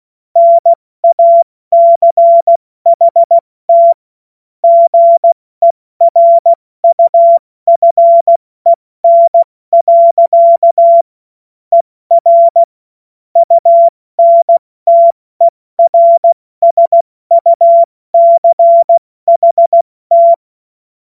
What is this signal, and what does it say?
Morse code, heard as a keyed tone, about 16 wpm, spaced as for 14 wpm, 680 Hz. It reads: NACHT GERUFEN. ER UNTERSUCHT
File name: Morsekurs / Witze / Witz 26 März 2024 70 F80